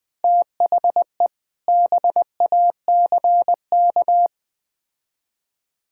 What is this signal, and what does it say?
Morse code, heard as a keyed tone, about 20 wpm, 700 Hz. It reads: THE BACK
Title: Build A Phrase - 3-Word Phrases - 20wpm